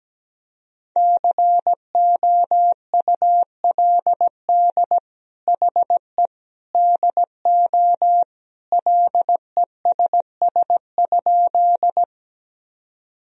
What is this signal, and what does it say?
Morse code, heard as a keyed tone, about 17 wpm, 700 Hz. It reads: COULD HE DO LESS?
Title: Sentences from Top 1000 Words - Review Part 2 - 17wpm